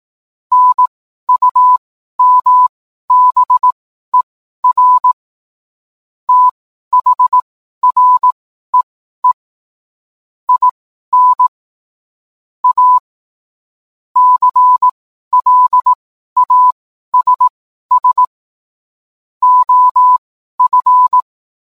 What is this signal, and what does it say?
Morse code, heard as a keyed tone, about 18 wpm, spaced as for 12 wpm, 1000 Hz. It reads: NUMBER THREE IN A CLASS OF
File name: cwblog / cw mp3 directory / mem morse 12